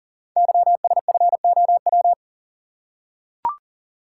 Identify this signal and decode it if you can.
Morse code, heard as a keyed tone, about 39 wpm, 700 Hz. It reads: YSFOW
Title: Code Groups - 5 Letter:Numbers - 40wpm